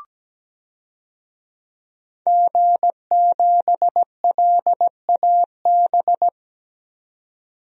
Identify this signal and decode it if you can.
Morse code, heard as a keyed tone, about 17 wpm, 700 Hz. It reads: G7LAB
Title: DXCC-100 Call Signs - 5 Characters - 17wpm